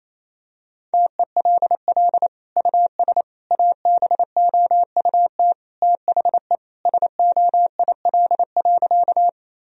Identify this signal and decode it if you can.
Morse code, heard as a keyed tone, about 28 wpm, 700 Hz. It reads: TELL US ABOUT THE SOIL.
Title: Sentences from Top 1000 Words - Review Part 4 - 28wpm